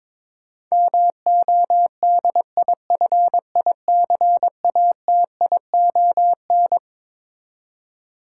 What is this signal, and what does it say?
Morse code, heard as a keyed tone, about 22 wpm, 700 Hz. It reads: MODIFICATION
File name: Twelve Letter Words - 22wpm